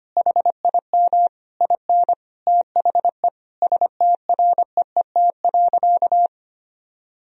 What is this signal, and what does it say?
Morse code, heard as a keyed tone, about 25 wpm, 700 Hz. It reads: HIM IN THE STREET.
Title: Sentences from Top 1000 Words - Review Part 3 - 25wpm